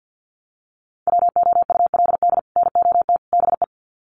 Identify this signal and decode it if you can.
Morse code, heard as a keyed tone, about 48 wpm, 700 Hz. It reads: WOULD NOT BE
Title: Sets of 3 Words 50wpm